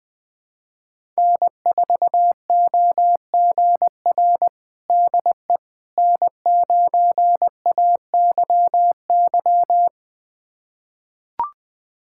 Morse code, heard as a keyed tone, about 21 wpm, 700 Hz. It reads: N4OGR DE N9AYY
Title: USA Callsign DE Callsign - 5 Characters - 20wpm